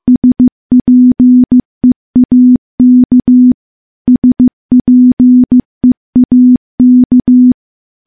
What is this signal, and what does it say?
Morse code, heard as a keyed tone, about 15 wpm, 250 Hz. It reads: SPEAK SPEAK
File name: morse / morse-training-15WPM